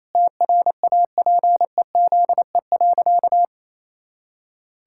Morse code, heard as a keyed tone, about 28 wpm, 700 Hz. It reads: TRAPEZE.
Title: Pangram Challenge v2 - 28wpm